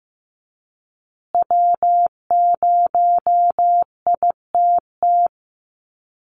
Morse code, heard as a keyed tone, about 15 wpm, 700 Hz. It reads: W0ITT
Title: Call Signs Repeated 3x - 15wpm